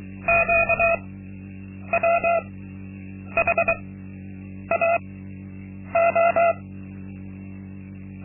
Morse code, heard as a keyed tone, about 23 wpm, spaced as for 8 wpm, 680 Hz. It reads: QWHAO
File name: Linksmakalnis-2